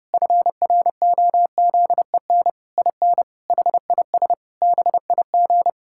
Morse code, heard as a keyed tone, about 32 wpm, 700 Hz. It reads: FROZEN IN HIS BIG
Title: Pangram Challenge v2 - 30wpm